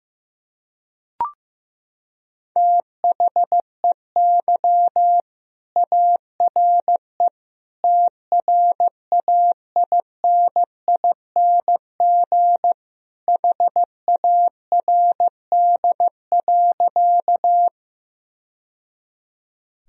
Morse code, heard as a keyed tone, about 15 wpm, 700 Hz. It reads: THEY ARE TRAINING HARD.
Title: Sentences from Top 1000 Words - Review Part 4 - 15wpm